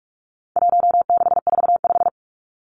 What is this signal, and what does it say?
Morse code, heard as a keyed tone, about 44 wpm, 700 Hz. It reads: 1645
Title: Numbers - 4-Digit 45wpm